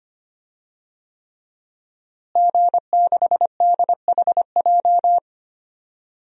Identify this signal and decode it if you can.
Morse code, heard as a keyed tone, about 25 wpm, 700 Hz. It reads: G6DHJ
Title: DXCC-100 Call Signs - 5 Characters - Repeated 3x - 25wpm